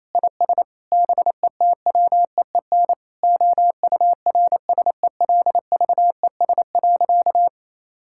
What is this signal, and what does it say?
Morse code, heard as a keyed tone, about 27 wpm, 700 Hz. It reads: IS BETWEEN OURSELVES.
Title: Sentences from Top 1600 Words 28wpm